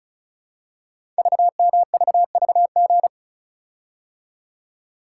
Morse code, heard as a keyed tone, about 36 wpm, 690 Hz. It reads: VMVVG
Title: Code Groups - 5 Letters - 35wpm